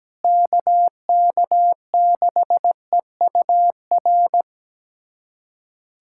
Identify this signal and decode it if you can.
Morse code, heard as a keyed tone, about 17 wpm, 700 Hz. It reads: KK6EUR
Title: USA Callsign DE Callsign - 6 Characters - 17wpm